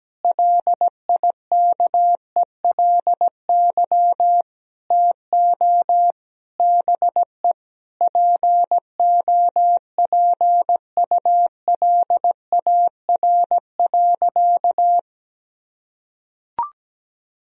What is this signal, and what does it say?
Morse code, heard as a keyed tone, about 17 wpm, 700 Hz. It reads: LIKELY TO BE POPULAR.
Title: Sentences from Top 1000 Words - Review Part 4 - 17wpm